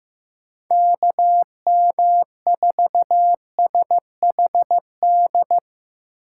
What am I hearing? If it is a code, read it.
Morse code, heard as a keyed tone, about 15 wpm, 700 Hz. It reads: KM4SHD
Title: USA Call Signs - 6 Characters - 6x Character Spacing - 15wpm